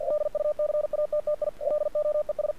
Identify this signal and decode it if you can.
Morse code, heard as a keyed tone, about 49 wpm, 600 Hz. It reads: DUCATTI LOES
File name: archived / CRSnet